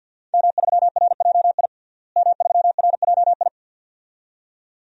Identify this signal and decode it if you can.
Morse code, heard as a keyed tone, about 52 wpm, 700 Hz. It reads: M3RJI M3RJI
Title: DXCC-100 Call Signs - 5 Characters - Repeated 2x - 50wpm